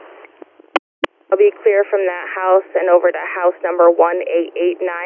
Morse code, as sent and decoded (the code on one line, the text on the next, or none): none